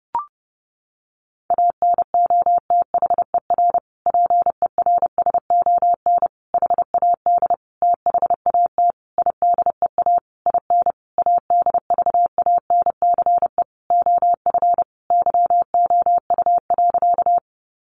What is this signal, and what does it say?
Morse code, heard as a keyed tone, about 30 wpm, 700 Hz. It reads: ANOTHER PERSON HAD THAT IDEA IN ADVANCE OF YOU.
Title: Sentences from Top 1400 Words 30wpm